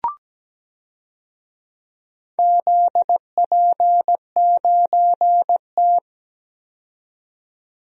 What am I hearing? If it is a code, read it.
Morse code, heard as a keyed tone, about 17 wpm, 700 Hz. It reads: ZP9T
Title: DXCC-100 Call Signs - 4 Characters - 17wpm